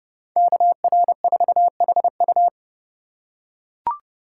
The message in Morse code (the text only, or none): KR4HU